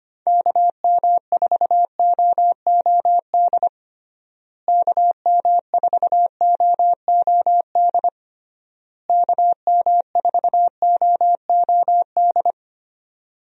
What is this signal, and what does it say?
Morse code, heard as a keyed tone, about 25 wpm, 700 Hz. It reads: KM4OOD KM4OOD KM4OOD
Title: USA Call Signs - 6 Characters - Repeated 3x - 25wpm